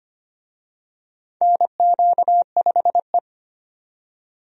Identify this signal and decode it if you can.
Morse code, heard as a keyed tone, about 25 wpm, 700 Hz. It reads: NQ5E
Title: USA Call Signs - 4 Characters - 25wpm